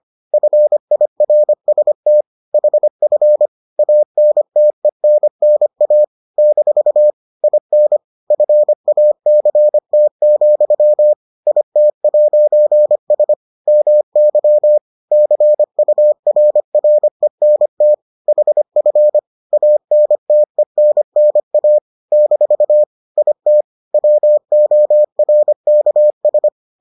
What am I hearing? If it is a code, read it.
Morse code, heard as a keyed tone, about 25 wpm, 600 Hz. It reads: FIRST HF ANTENNA - IN FACT, IT'S MY CURRENT HF ANTENNA - IT WORKS